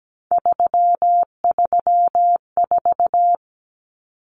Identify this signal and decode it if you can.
Morse code, heard as a keyed tone, about 17 wpm, 700 Hz. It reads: 334
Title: Numbers - 3-Digit 17wpm